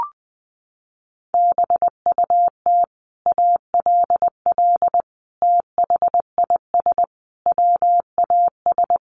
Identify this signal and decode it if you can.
Morse code, heard as a keyed tone, about 20 wpm, 700 Hz. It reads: BUT ALL THIS WAS
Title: Sentences from Top 1400 Words 20wpm